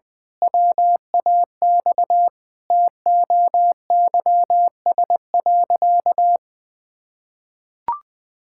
Morse code, heard as a keyed tone, about 20 wpm, 700 Hz. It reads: WAX TOYS.
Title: Pangram Challenge v2 - 20wpm